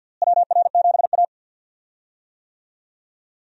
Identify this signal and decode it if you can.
Morse code, heard as a keyed tone, about 52 wpm, 700 Hz. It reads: WR7A
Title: USA Call Signs - 4 Characters - Repeated 2x - 50wpm